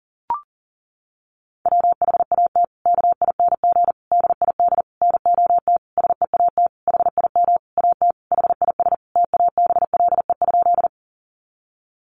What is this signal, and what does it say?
Morse code, heard as a keyed tone, about 40 wpm, 700 Hz. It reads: WHAT KING DID NOT SEAT HIM AT HIS TABLE?
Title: Sentences from Top 1500 Words 40wpm